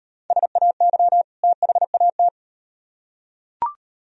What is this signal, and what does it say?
Morse code, heard as a keyed tone, about 36 wpm, 700 Hz. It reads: SAY THAT